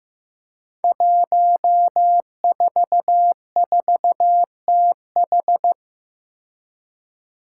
Morse code, heard as a keyed tone, about 15 wpm, 700 Hz. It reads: 144TH
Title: Five Challenging Letter-Number Combinations 15wpm